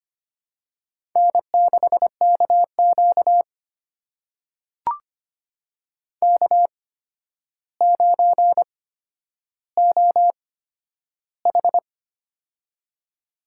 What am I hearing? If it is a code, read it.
Morse code, heard as a keyed tone, about 25 wpm, spaced as for 11 wpm, 700 Hz. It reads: N6KQ K9OH
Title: USA Call Signs - 4 Characters - 8x Character Spacing - 25wpm